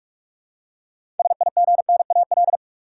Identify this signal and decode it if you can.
Morse code, heard as a keyed tone, about 46 wpm, 700 Hz. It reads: SIGNAL
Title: Six Letter Words - 45wpm